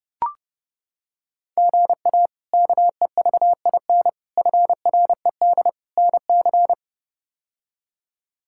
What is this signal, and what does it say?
Morse code, heard as a keyed tone, about 30 wpm, 700 Hz. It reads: GA KEVIN FRED NC
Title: K1USN SST QSO Practice - 30wpm